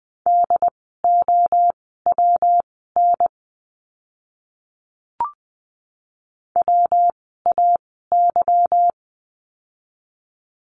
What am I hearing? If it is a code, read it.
Morse code, heard as a keyed tone, about 20 wpm, spaced as for 15 wpm, 700 Hz. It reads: DOWN WAY
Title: 244 Lesson 39f - Letter z - Words 20wpm